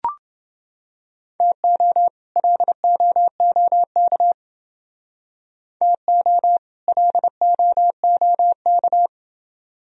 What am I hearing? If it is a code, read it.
Morse code, heard as a keyed tone, about 29 wpm, 700 Hz. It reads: TO LOOK TO LOOK